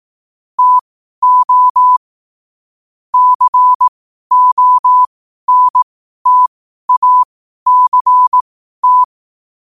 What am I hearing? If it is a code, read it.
Morse code, heard as a keyed tone, about 18 wpm, spaced as for 12 wpm, 1000 Hz. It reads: TO CONTACT
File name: qotdfiles / 12wpm